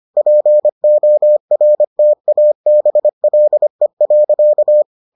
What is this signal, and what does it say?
Morse code, heard as a keyed tone, about 25 wpm, 600 Hz. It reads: PORTABLE.